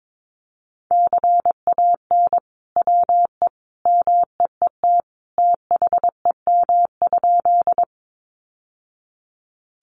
Morse code, heard as a keyed tone, about 22 wpm, 700 Hz. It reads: CAN WE MEET THEM?
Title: Sentences from Top 1100 Words 22wpm